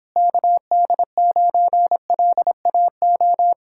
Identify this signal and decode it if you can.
Morse code, heard as a keyed tone, about 26 wpm, 700 Hz. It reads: KD9LAO